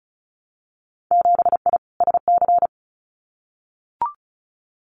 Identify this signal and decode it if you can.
Morse code, heard as a keyed tone, about 36 wpm, 700 Hz. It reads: 7I SC
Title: Winter Field Day Exchanges Encore 35wpm